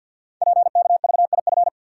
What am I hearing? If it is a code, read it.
Morse code, heard as a keyed tone, about 48 wpm, 690 Hz. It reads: PKVIF
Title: Code Groups - 5 Letters - 50wpm